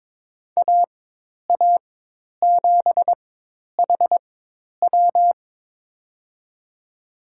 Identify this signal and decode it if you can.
Morse code, heard as a keyed tone, about 22 wpm, spaced as for 10 wpm, 700 Hz. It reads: AA7HW
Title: USA Call Signs - 5 Characters - 4x Character Spacing - 22wpm